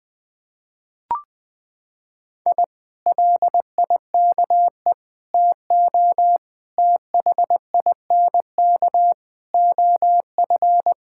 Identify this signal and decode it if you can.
Morse code, heard as a keyed tone, about 21 wpm, 700 Hz. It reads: I LIKE TO THINK OF
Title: Sentences with 8-Word Phrases - 20wpm